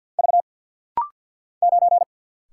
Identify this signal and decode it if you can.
Morse code, heard as a keyed tone, about 50 wpm, 700 Hz. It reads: V 9
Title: Single Letter-Number - Rapid-Fire 50wpm